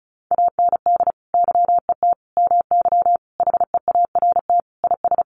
Morse code, heard as a keyed tone, about 34 wpm, 700 Hz. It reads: AND YET MY HEART IS
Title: Sentences from Top 1200 Words 35wpm